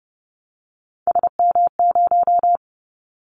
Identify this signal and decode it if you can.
Morse code, heard as a keyed tone, about 30 wpm, 700 Hz. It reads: SM0
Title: Callsign Prefixes for Top 25 DXCC - 30wpm